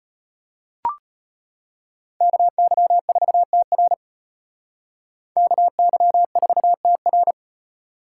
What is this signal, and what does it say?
Morse code, heard as a keyed tone, about 33 wpm, 700 Hz. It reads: KY4TR KY4TR